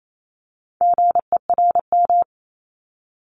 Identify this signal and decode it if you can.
Morse code, heard as a keyed tone, about 28 wpm, 700 Hz. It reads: GERM
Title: Multi-letter Phonemes 28wpm